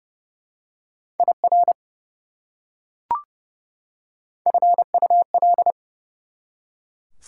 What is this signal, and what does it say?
Morse code, heard as a keyed tone, about 31 wpm, 700 Hz. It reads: IR FUL
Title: Prefixes and Suffixes 30wpm